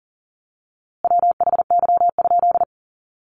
Morse code, heard as a keyed tone, about 40 wpm, 700 Hz. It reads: WHY?
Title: A Sound Of Thunder 40wpm